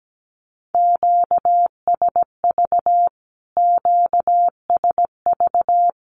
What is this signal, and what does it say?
Morse code, heard as a keyed tone, about 17 wpm, 700 Hz. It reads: QSV QSV
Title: Q-Codes Repeated 3x 17wpm